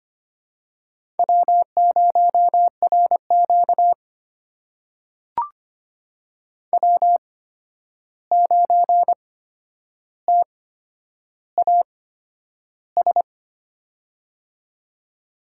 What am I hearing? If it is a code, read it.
Morse code, heard as a keyed tone, about 25 wpm, spaced as for 10 wpm, 700 Hz. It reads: W0RQ W9TAS
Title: USA Call Signs - 8x Character Spacing - 25wpm